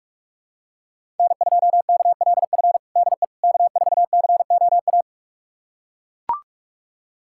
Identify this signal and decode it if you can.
Morse code, heard as a keyed tone, about 42 wpm, 700 Hz. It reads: N2KLF DE K4COA
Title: USA Callsign DE Callsign - 5 Characters - 45wpm